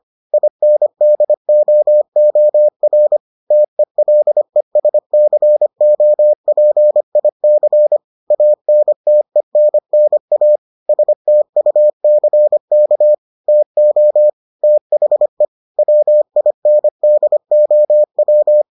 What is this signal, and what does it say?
Morse code, heard as a keyed tone, about 25 wpm, 600 Hz. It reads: INDOOR TELESCOPIC ANTENNA STUCK TO THE WINDOW